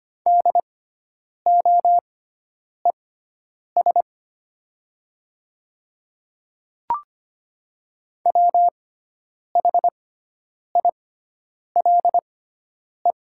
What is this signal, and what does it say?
Morse code, heard as a keyed tone, about 25 wpm, spaced as for 9 wpm, 700 Hz. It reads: DOES WHILE
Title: 6x Character Spacing - Top 500 Words - 25wpm